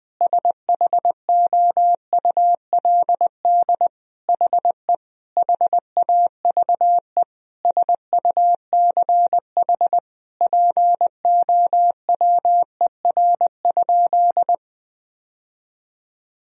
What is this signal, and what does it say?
Morse code, heard as a keyed tone, about 20 wpm, 700 Hz. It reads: SHOULD HE HAVE SUCH POWER?
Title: Sentences from Top 1000 Words - Review Part 2 - 20wpm